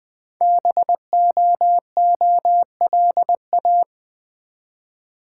Morse code, heard as a keyed tone, about 20 wpm, 700 Hz. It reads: BOOLA
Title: Code Groups - 5 Letters - 20wpm